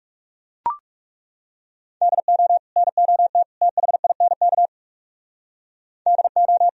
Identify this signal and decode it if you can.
Morse code, heard as a keyed tone, about 38 wpm, 700 Hz. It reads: DO NOT THINK DO